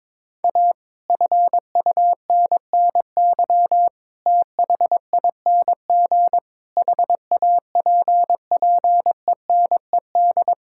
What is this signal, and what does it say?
Morse code, heard as a keyed tone, about 22 wpm, 700 Hz. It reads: A FUNNY THING HAPPENED
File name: practice / 22wpm / Sets of 8 Words - 22wpm